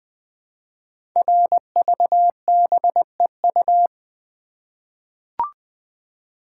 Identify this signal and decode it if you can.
Morse code, heard as a keyed tone, about 20 wpm, 700 Hz. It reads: RVBEU
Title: Code Groups - 5 Letters - 20wpm